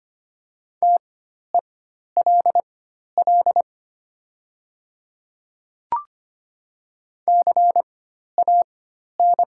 Morse code, heard as a keyed tone, about 25 wpm, spaced as for 12 wpm, 700 Hz. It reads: TELL CAN
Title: 4x Character Spacing - Top 500 Words - 25wpm